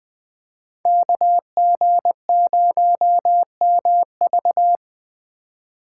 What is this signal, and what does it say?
Morse code, heard as a keyed tone, about 24 wpm, 700 Hz. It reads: KG0MV
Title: Speed Racing - USA 2x2 Call Signs 20wpm